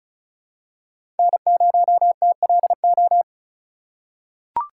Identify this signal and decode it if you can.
Morse code, heard as a keyed tone, about 36 wpm, 700 Hz. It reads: N0TLO
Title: USA Call Signs - 5 Characters - 2x Character Spacing - 35wpm